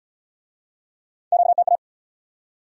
Morse code, heard as a keyed tone, about 75 wpm, 700 Hz. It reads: QTA